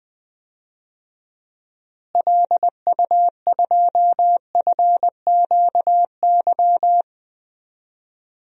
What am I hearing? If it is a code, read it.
Morse code, heard as a keyed tone, about 21 wpm, 700 Hz. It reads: LU2FQY
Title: DXCC-100 Call Signs - 6 Characters - Repeated 2x - 20wpm